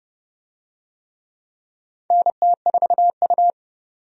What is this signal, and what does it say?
Morse code, heard as a keyed tone, about 30 wpm, 700 Hz. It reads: NT4U
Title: USA Call Signs - 4 Characters - 30wpm